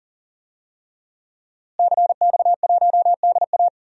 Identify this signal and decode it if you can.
Morse code, heard as a keyed tone, about 39 wpm, 700 Hz. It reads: CX1DA
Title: DXCC-100 Call Signs - 5 Characters - Repeated 3x - 40wpm